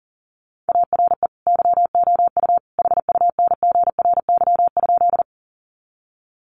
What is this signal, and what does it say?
Morse code, heard as a keyed tone, about 40 wpm, 700 Hz. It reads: ARE YOU HUNGRY?
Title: Sentences from Top 1400 Words 40wpm